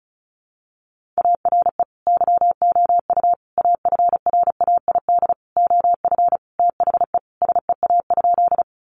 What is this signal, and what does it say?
Morse code, heard as a keyed tone, about 34 wpm, 700 Hz. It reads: ARE YOU AFRAID OF THE SEA?
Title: Sentences from Top 1500 Words 35wpm